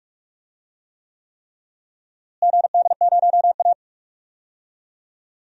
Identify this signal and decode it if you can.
Morse code, heard as a keyed tone, about 41 wpm, 690 Hz. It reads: GD0A
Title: DXCC-100 Call Signs - 4 Characters - Repeated 3x - 45wpm